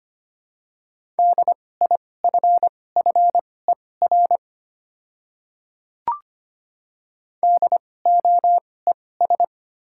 Morse code, heard as a keyed tone, about 25 wpm, spaced as for 18 wpm, 690 Hz. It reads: DIFFER DOES
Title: 2x Character Spacing - Top 300 Words - 25wpm